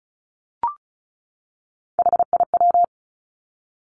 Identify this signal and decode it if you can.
Morse code, heard as a keyed tone, about 36 wpm, 700 Hz. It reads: HIW
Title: Three Challenging Letter-Number Combinations 35wpm